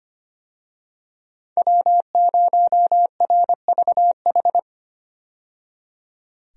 Morse code, heard as a keyed tone, about 25 wpm, 700 Hz. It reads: W0RVH
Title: USA Call Signs - 5 Characters - 25wpm